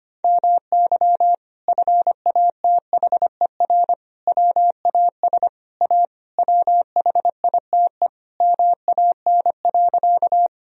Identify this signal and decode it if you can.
Morse code, heard as a keyed tone, about 25 wpm, 700 Hz. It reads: MY FATHER WAS A WHITE MAN.
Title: Sentences from Top 1000 Words - Review Part 1 - 25wpm